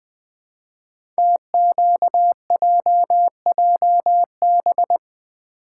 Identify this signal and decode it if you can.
Morse code, heard as a keyed tone, about 20 wpm, 700 Hz. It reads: TQJJB
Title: Code Groups - 5 Letter:Numbers - 20wpm